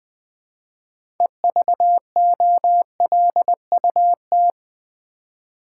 Morse code, heard as a keyed tone, about 20 wpm, 700 Hz. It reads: EVOLUT
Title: Build A Word - 6 Letters At A Time - 12-Letter Words - 20wpm